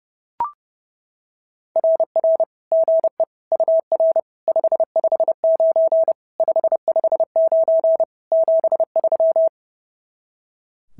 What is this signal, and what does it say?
Morse code, heard as a keyed tone, about 28 wpm, 650 Hz. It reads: RR GE UR 559 559 73
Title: SOTA QSO Practice 30wpm